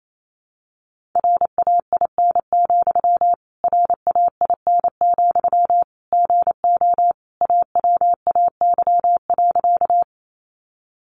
Morse code, heard as a keyed tone, about 28 wpm, 700 Hz. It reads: RAIN, RAIN, GO AWAY.
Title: Sentences from Top 600 Words 28wpm